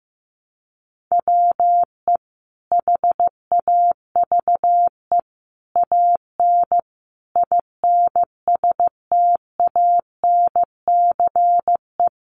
Morse code, heard as a keyed tone, about 15 wpm, 700 Hz. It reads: WE HAVE AN INSTANCE
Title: Sentences from Top 800 Words 15wpm